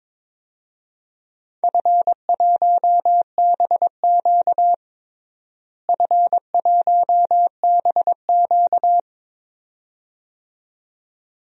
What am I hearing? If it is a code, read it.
Morse code, heard as a keyed tone, about 22 wpm, 700 Hz. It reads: F1BQ F1BQ
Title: DXCC-100 Call Signs - 4 Characters - Repeated 2x - 22wpm